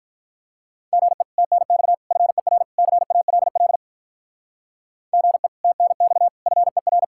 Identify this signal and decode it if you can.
Morse code, heard as a keyed tone, about 46 wpm, 700 Hz. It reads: GE TNX FER CALL GE TNX FER